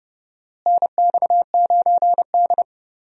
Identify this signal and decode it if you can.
Morse code, heard as a keyed tone, about 30 wpm, 700 Hz. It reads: NX9D
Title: USA Call Signs - 4 Characters - Repeated 2x - 30wpm